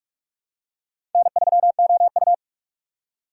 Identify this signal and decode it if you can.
Morse code, heard as a keyed tone, about 46 wpm, 700 Hz. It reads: N3OU
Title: USA Call Signs - 6x Character Spacing - 45wpm